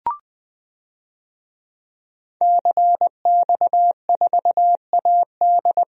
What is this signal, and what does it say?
Morse code, heard as a keyed tone, about 20 wpm, 700 Hz. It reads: CX4AD
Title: DXCC-100 Call Signs - 5 Characters - 20wpm